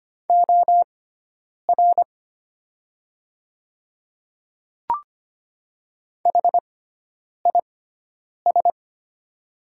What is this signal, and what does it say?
Morse code, heard as a keyed tone, about 25 wpm, spaced as for 9 wpm, 700 Hz. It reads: OR HIS
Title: 6x Character Spacing - Top 100 Words - 25wpm